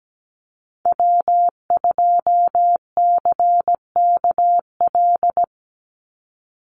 Morse code, heard as a keyed tone, about 17 wpm, 700 Hz. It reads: W2CKL
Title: NA SOTA Activator Callsigns 17wpm